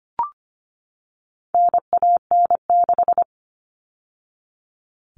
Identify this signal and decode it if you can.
Morse code, heard as a keyed tone, about 25 wpm, 700 Hz. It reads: NAN6
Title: All Cut Numbers - 4-Digit 25wpm